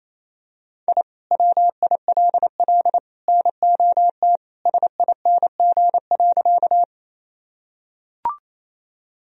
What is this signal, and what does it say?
Morse code, heard as a keyed tone, about 28 wpm, 700 Hz. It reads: I WILL NOT SING.
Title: Sentences from Top 1000 Words - Review Part 2 - 28wpm